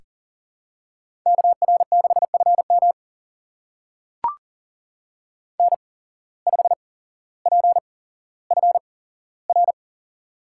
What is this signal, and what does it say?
Morse code, heard as a keyed tone, about 40 wpm, spaced as for 17 wpm, 700 Hz. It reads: KR6FM N5PFR
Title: USA Call Signs - 5 Characters - 8x Character Spacing - 40wpm